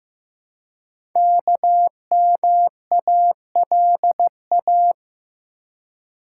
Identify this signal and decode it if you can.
Morse code, heard as a keyed tone, about 15 wpm, 700 Hz. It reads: KMALA
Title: Code Groups - 5 Letters - 15wpm